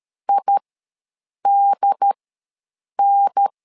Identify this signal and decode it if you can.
Morse code, heard as a keyed tone, about 13 wpm, spaced as for 7 wpm, 800 Hz. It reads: IDN
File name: LEZIONI / 2 UHDN / 2 UHDN ws 12wpm 1708